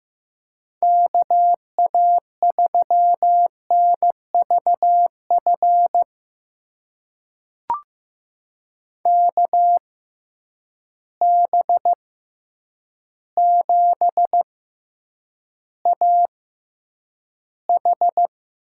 Morse code, heard as a keyed tone, about 15 wpm, spaced as for 8 wpm, 700 Hz. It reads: KA3NVF KB7AH
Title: USA Call Signs - 6 Characters - 6x Character Spacing - 15wpm